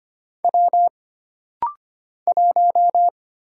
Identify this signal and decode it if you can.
Morse code, heard as a keyed tone, about 25 wpm, 700 Hz. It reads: W 1
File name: Single Character (Letter:Number:Punctuation:Prosign) - Rapid-Fire - 25wpm